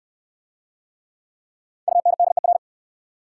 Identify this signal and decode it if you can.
Morse code, heard as a keyed tone, about 61 wpm, 690 Hz. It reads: UNDER